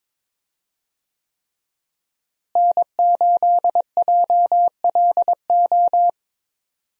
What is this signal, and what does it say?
Morse code, heard as a keyed tone, about 21 wpm, 700 Hz. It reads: N8JLO
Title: USA Call Signs - 5 Characters - Repeated 3x - 22wpm